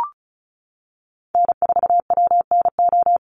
Morse code, heard as a keyed tone, about 34 wpm, 700 Hz. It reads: N4WNO
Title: USA Call Signs Repeated 2x 35wpm